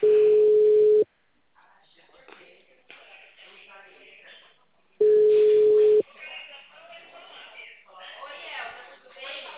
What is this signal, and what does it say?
Ringback tone (Europe, 425 Hz), with speech.